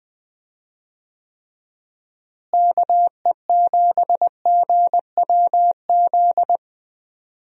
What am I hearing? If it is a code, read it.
Morse code, heard as a keyed tone, about 20 wpm, 700 Hz. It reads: KE7GWZ
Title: USA Call Signs - 6 Characters - Repeated 3x - 20wpm